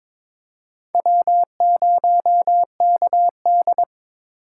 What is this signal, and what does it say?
Morse code, heard as a keyed tone, about 22 wpm, 700 Hz. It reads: W0KD
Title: USA Call Signs - 6x Character Spacing - 22wpm